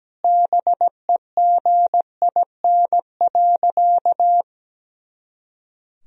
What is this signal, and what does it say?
Morse code, heard as a keyed tone, about 17 wpm, 700 Hz. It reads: BEGIN.
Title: Sentences from Top 300 Words and Abbreviations 17wpm